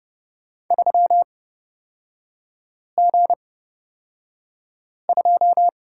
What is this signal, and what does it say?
Morse code, heard as a keyed tone, about 31 wpm, spaced as for 5 wpm, 700 Hz. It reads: 3G2
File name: Lesson 38e - Letter q - Three Characters 30wpm